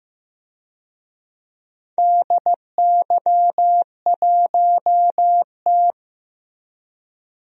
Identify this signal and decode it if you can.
Morse code, heard as a keyed tone, about 15 wpm, 700 Hz. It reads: DY1T
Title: DXCC-100 Call Signs - 4 Characters - Repeated 3x - 15wpm